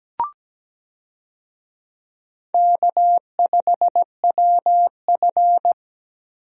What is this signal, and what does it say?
Morse code, heard as a keyed tone, about 17 wpm, 700 Hz. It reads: K5WF
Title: USA Call Signs - 4 Characters - Repeated 3x - 17wpm